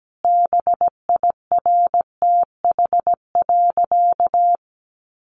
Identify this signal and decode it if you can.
Morse code, heard as a keyed tone, about 17 wpm, 700 Hz. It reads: BIRTH.
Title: Sentences from Top 1300 Words 17wpm